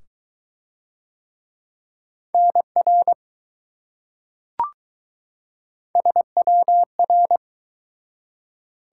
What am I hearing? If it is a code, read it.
Morse code, heard as a keyed tone, about 21 wpm, 700 Hz. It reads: NR SWR